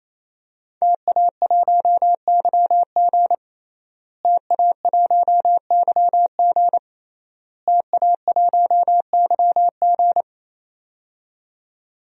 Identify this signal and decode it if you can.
Morse code, heard as a keyed tone, about 28 wpm, 700 Hz. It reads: TA1YG TA1YG TA1YG
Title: DXCC-100 Call Signs - 5 Characters - Repeated 3x - 28wpm